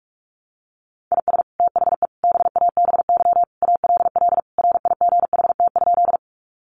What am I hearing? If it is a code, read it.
Morse code, heard as a keyed tone, about 44 wpm, 700 Hz. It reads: IS THE BABY ALL RIGHT?
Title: Sentences from Top 1300 Words 45wpm